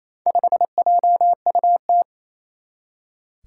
Morse code, heard as a keyed tone, about 28 wpm, 700 Hz. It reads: 5JUT
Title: Four Challenging Letter-Number Combinations 28wpm